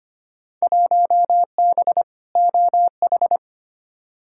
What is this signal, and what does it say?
Morse code, heard as a keyed tone, about 25 wpm, 700 Hz. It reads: 1B OH
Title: Field Day Exchanges - 25wpm